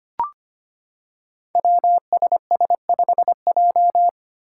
Morse code, heard as a keyed tone, about 25 wpm, 700 Hz. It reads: WSS5J
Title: Five Challenging Letter-Number Combinations 25wpm